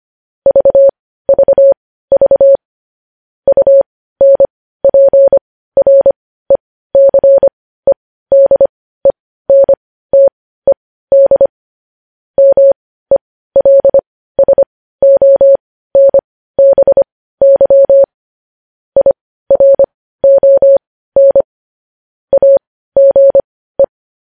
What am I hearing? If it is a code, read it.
Morse code, heard as a keyed tone, about 25 wpm, spaced as for 15 wpm, 550 Hz. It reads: VVV UNPRECEDENTED MELSONBY IRON AGE
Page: news-headlines-in-morse-code-at-15-wpm